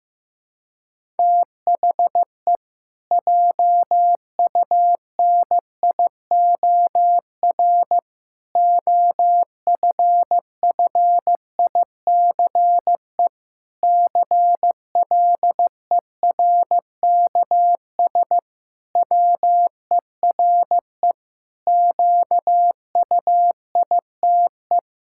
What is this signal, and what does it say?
Morse code, heard as a keyed tone, about 15 wpm, 700 Hz. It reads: THE JUNIOR OFFICE CLERKS WERE QUITE
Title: Pangram Challenge v2 - 15wpm